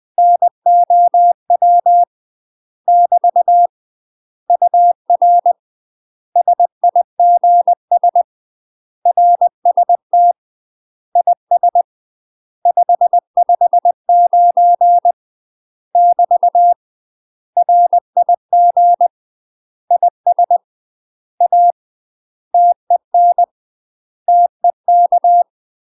Morse code, heard as a keyed tone, about 20 wpm, 700 Hz. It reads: NOW <BT> UR SIGS RST IS 559 <BT> RIG IS A TEN TEK
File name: UK MASTER QSOs TRAINING/20WPM